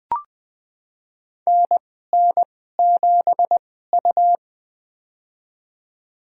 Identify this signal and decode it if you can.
Morse code, heard as a keyed tone, about 20 wpm, spaced as for 14 wpm, 700 Hz. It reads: NN7U
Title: USA Call Signs - 4 Characters - 2x Character Spacing - 20wpm